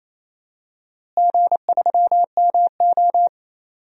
Morse code, heard as a keyed tone, about 28 wpm, 700 Hz. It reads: G3MO
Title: DXCC-100 Call Signs - 4 Characters - Repeated 3x - 28wpm